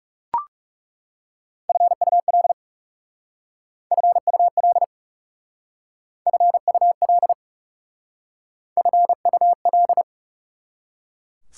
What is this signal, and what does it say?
Morse code, heard as a keyed tone, about 38 wpm, 700 Hz. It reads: FUL FUL FUL FUL